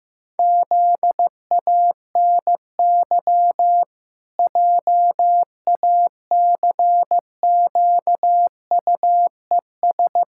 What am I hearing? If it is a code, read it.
Morse code, heard as a keyed tone, about 15 wpm, 700 Hz. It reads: ZANY JACQUES
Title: Pangram Challenge v2 - 15wpm